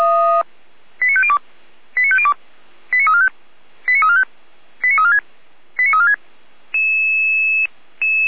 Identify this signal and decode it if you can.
Five-tone selective call (ZVEI), heard as quick runs of five tones, tones of about 70 ms. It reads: 89502 89502 89345 89345 89346 89346